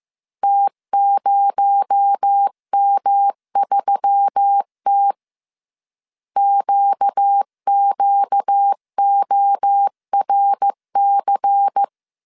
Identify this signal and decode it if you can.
Morse code, heard as a keyed tone, about 15 wpm, 800 Hz. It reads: T0M3T QQORC
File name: CW / DET / 1 6 1500C S C 14wpm 2951